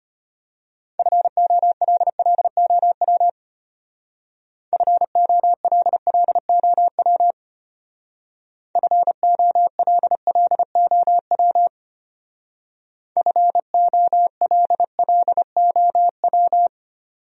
Morse code, heard as a keyed tone, about 30 wpm, 700 Hz. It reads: FOLLOW FOLLOW FOLLOW FOLLOW
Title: Speed-Racing Top 500 Words 25wpm